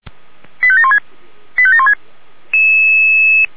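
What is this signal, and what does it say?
Five-tone selective call (ZVEI), heard as quick runs of five tones, tones of about 70 ms. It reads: 85716 85716
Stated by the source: ZVEI five-tone selective call to address 89291, received via SDR.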